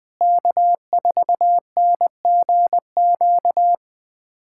Morse code, heard as a keyed tone, about 20 wpm, 700 Hz. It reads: K4NGQ
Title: USA Callsign DE Callsign - 5 Characters - 20wpm